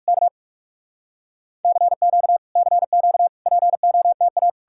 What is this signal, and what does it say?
Morse code, heard as a keyed tone, about 59 wpm, spaced as for 51 wpm, 700 Hz. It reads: K CQ CQ POTA
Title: Speed-Racing Common QSO Phrases - 45wpm